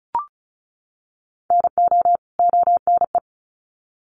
Morse code, heard as a keyed tone, about 34 wpm, 700 Hz. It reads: NO ONE
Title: Sets of 2 Words 35wpm